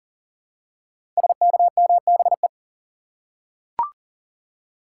Morse code, heard as a keyed tone, about 38 wpm, 700 Hz. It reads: SKMBE
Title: Code Groups - 5 Letters - 40wpm